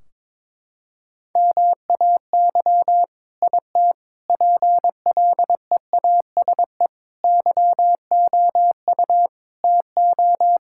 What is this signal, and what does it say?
Morse code, heard as a keyed tone, about 22 wpm, 700 Hz. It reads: MAY IT PLEASE YOU TO
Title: Sentences from Top 1600 Words 22wpm